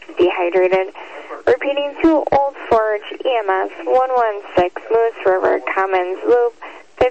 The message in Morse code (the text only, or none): none